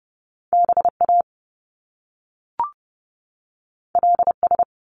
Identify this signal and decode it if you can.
Morse code, heard as a keyed tone, about 30 wpm, 700 Hz. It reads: BA LS
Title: Top Two Letter Combinations in Words 30wpm